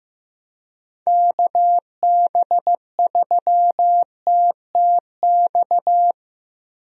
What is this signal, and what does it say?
Morse code, heard as a keyed tone, about 16 wpm, 700 Hz. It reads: KB3TTX
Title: USA Call Signs - 6 Characters - 15wpm